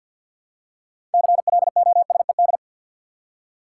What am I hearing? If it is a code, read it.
Morse code, heard as a keyed tone, about 53 wpm, 700 Hz. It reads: CLOSED
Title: Six Letter Words - 50wpm